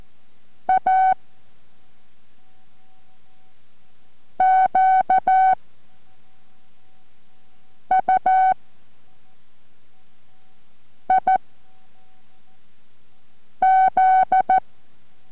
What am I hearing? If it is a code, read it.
Morse code, heard as a keyed tone, about 14 wpm, spaced as for 3 wpm, 750 Hz. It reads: A QUIZ